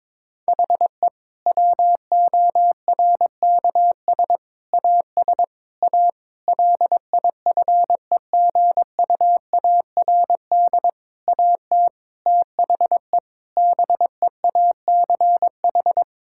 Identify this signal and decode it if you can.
Morse code, heard as a keyed tone, about 21 wpm, 700 Hz. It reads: HE WORKS AS A LIFEGUARD AT THE BEACH
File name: Difficult Sentences with Hints - 22wpm